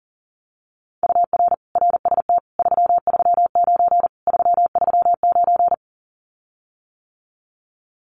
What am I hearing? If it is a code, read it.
Morse code, heard as a keyed tone, about 40 wpm, 700 Hz. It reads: UR RST 339 339
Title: Common QSO Phrases Encore 40wpm